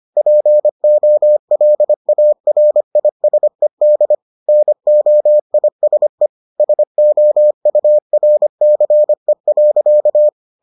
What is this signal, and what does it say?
Morse code, heard as a keyed tone, about 25 wpm, 600 Hz. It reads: POLARISED NOISE SOURCE.